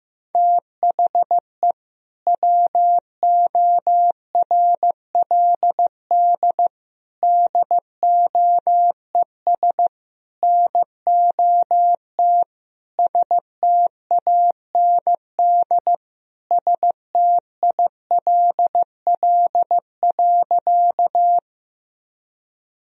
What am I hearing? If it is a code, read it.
Morse code, heard as a keyed tone, about 15 wpm, 700 Hz. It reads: THE WORLD DOES NOT STAND STILL.
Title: Sentences from Top 1000 Words - Review Part 1 - 15wpm